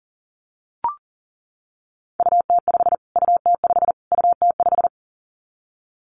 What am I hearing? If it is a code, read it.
Morse code, heard as a keyed tone, about 40 wpm, 700 Hz. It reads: UT5 UT5 UT5
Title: Callsign Prefixes for Top 25 DXCC - Repeated 3x - 40wpm